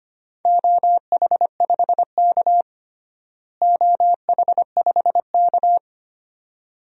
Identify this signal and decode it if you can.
Morse code, heard as a keyed tone, about 25 wpm, 700 Hz. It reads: OH5K OH5K
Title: DXCC-100 Call Signs - 4 Characters - Repeated 2x - 25wpm